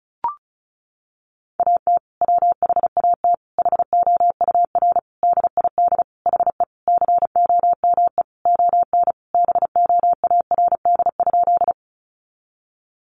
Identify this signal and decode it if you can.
Morse code, heard as a keyed tone, about 34 wpm, 700 Hz. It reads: AT WHAT HOUR DID HE COME ON BOARD?
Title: Sentences from Top 700 Words 35wpm